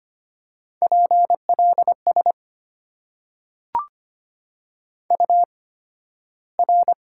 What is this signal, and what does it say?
Morse code, heard as a keyed tone, about 25 wpm, spaced as for 13 wpm, 690 Hz. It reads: PLS UR
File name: Top 90 QSO Elements - 8x Character Spacing - 25wpm